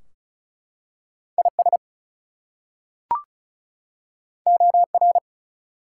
Morse code, heard as a keyed tone, about 36 wpm, 700 Hz. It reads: IS OR
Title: Two Letter Words - 35wpm